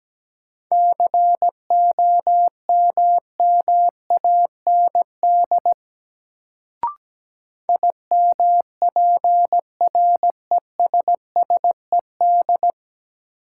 Morse code, heard as a keyed tone, about 17 wpm, 700 Hz. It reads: COMMAND IMPRESSED
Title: Sentences from Top 1600 Words - New Word Drill Alt - 17wpm